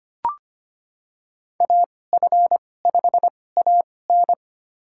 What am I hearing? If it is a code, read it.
Morse code, heard as a keyed tone, about 25 wpm, spaced as for 18 wpm, 700 Hz. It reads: AF5AN
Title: USA Call Signs - 5 Characters - 2x Character Spacing - 25wpm